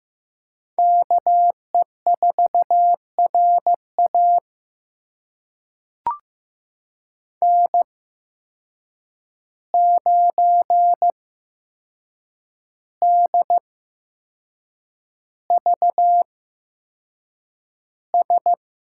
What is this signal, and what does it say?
Morse code, heard as a keyed tone, about 15 wpm, spaced as for 6 wpm, 700 Hz. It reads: KE4RA N9DVS
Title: USA Call Signs - 5 Characters - 8x Character Spacing - 15wpm